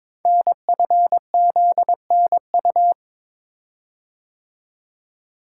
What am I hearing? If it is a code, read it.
Morse code, heard as a keyed tone, about 22 wpm, 700 Hz. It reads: NFZNU
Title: Code Groups - 5 Letters - 22wpm